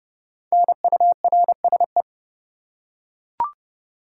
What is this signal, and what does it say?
Morse code, heard as a keyed tone, about 29 wpm, 700 Hz. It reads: NURSE
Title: Five Letter Words - 30wpm